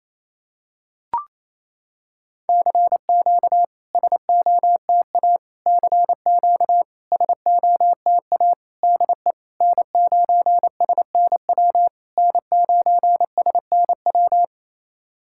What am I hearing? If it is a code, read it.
Morse code, heard as a keyed tone, about 28 wpm, 700 Hz. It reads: CQ SOTA CQ SOTA DE N9SNW N9SNW
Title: SOTA QSO Practice 28wpm